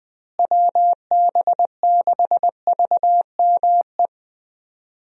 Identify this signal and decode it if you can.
Morse code, heard as a keyed tone, about 20 wpm, 700 Hz. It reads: WB6VME
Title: USA Call Signs - 6 Characters - Repeated 2x - 20wpm